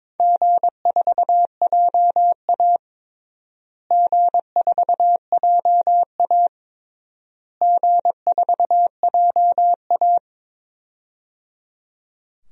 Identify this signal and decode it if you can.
Morse code, heard as a keyed tone, about 22 wpm, 700 Hz. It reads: G4JA G4JA G4JA
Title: DXCC-100 Call Signs - 4 Characters - Repeated 3x - 22wpm